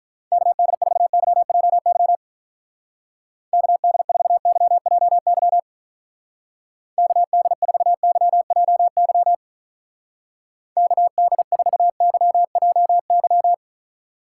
Morse code, heard as a keyed tone, about 42 wpm, 700 Hz. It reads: KD4YJY KD4YJY KD4YJY KD4YJY
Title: Speed Racing - USA Call Signs 35wpm